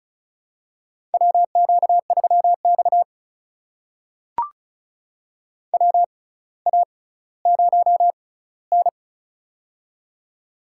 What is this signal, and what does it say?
Morse code, heard as a keyed tone, about 34 wpm, spaced as for 18 wpm, 700 Hz. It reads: WQ3X WA0N
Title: USA Call Signs - 4 Characters - 6x Character Spacing - 35wpm